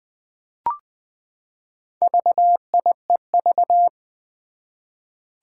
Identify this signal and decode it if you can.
Morse code, heard as a keyed tone, about 20 wpm, 700 Hz. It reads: VIEV
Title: Four Challenging Letter-Number Combinations 20wpm